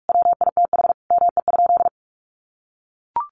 Morse code, heard as a keyed tone, about 44 wpm, 700 Hz. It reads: WITH ME?
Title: Sentences from Top 1500 Words 45wpm